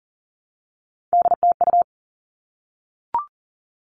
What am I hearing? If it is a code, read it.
Morse code, heard as a keyed tone, about 40 wpm, 700 Hz. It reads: DTU
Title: All Cut Numbers - 3-Digit 40wpm